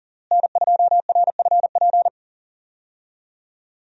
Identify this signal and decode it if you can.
Morse code, heard as a keyed tone, about 39 wpm, 690 Hz. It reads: N2RFP
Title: USA Call Signs - 5 Characters - Repeated 3x - 40wpm